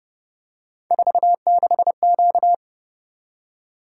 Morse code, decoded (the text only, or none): 46Q